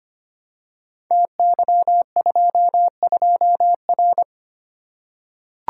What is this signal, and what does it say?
Morse code, heard as a keyed tone, about 25 wpm, 700 Hz. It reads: TY22R
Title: Code Groups - 5 Letter:Numbers - 25wpm